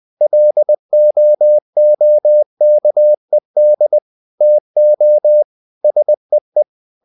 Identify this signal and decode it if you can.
Morse code, heard as a keyed tone, about 20 wpm, 600 Hz. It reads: LOOKED TO SEE
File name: the old man and the sea part 2 20wpm